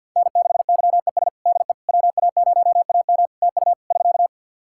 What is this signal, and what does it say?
Morse code, heard as a keyed tone, about 52 wpm, 700 Hz. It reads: N6YES DE WA0AM TU <SK>
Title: Practice QSO - Exchange 4 - Ending - 50wpm